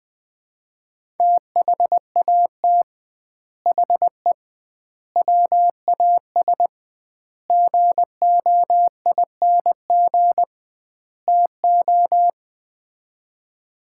Morse code, heard as a keyed tone, about 20 wpm, 700 Hz. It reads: THAT HE WAS GOING TO
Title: 2x Word-Spacing - Sets of 5 Words - 20wpm